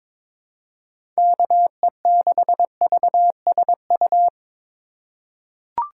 Morse code, heard as a keyed tone, about 22 wpm, 700 Hz. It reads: KE6VSU
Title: USA Call Signs - 4x Character Spacing - 22wpm